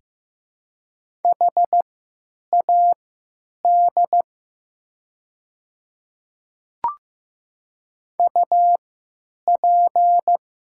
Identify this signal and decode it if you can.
Morse code, heard as a keyed tone, about 15 wpm, spaced as for 9 wpm, 700 Hz. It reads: HAD UP
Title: 3x Character Spacing - Top 300 Words - 15wpm